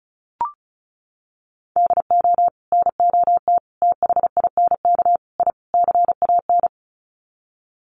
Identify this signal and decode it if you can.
Morse code, heard as a keyed tone, about 34 wpm, 700 Hz. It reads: DO NOT THINK I CAN
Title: Sets of 5 Words Encore2 35wpm